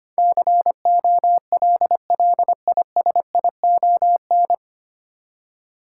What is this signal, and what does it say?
Morse code, heard as a keyed tone, about 25 wpm, 700 Hz. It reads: COLLISION
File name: Nine Letter Words - 25wpm